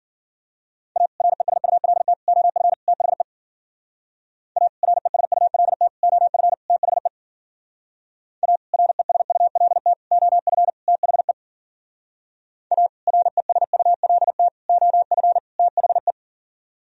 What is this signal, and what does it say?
Morse code, heard as a keyed tone, about 51 wpm, 700 Hz. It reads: A RESULT OF THE A RESULT OF THE A RESULT OF THE A RESULT OF THE